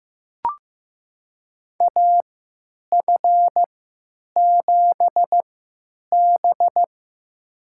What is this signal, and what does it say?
Morse code, heard as a keyed tone, about 15 wpm, spaced as for 9 wpm, 700 Hz. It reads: AF7B
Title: USA Call Signs - 4 Characters - 3x Character Spacing - 15wpm